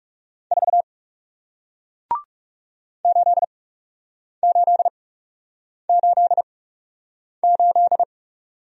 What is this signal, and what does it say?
Morse code, heard as a keyed tone, about 40 wpm, spaced as for 8 wpm, 700 Hz. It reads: 4 8888